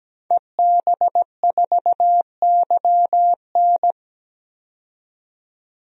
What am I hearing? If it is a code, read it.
Morse code, heard as a keyed tone, about 17 wpm, 700 Hz. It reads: EB4YN
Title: DXCC-100 Call Signs - 5 Characters - 17wpm